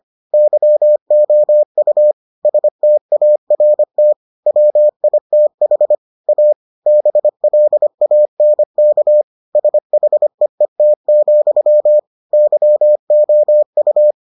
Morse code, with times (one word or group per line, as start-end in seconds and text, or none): 0.3-2.1 YOU
2.5-4.1 START
4.5-6.0 WITH
6.3-6.5 A
6.9-9.2 BLANK
9.6-12.0 SHEET,
12.3-14.1 YOU